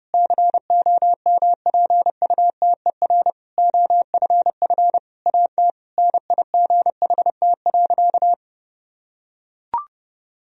Morse code, heard as a keyed tone, about 30 wpm, 700 Hz. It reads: COMPUTER OFF AT NIGHT.
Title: Sentences from Top 1000 Words - Review Part 3 - 30wpm